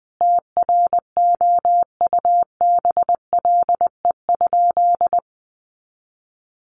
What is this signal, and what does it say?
Morse code, heard as a keyed tone, about 20 wpm, 700 Hz. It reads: TROUBLE?
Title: Sentences from Top 1200 Words 20wpm